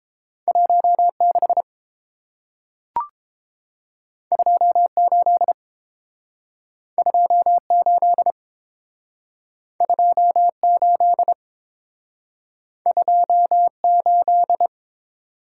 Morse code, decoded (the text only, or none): 16 28 28 28 28